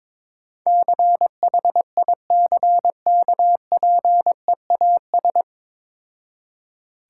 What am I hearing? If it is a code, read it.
Morse code, heard as a keyed tone, about 22 wpm, 690 Hz. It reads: CHICKPEAS
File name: Difficult Words with Hints - 22wpm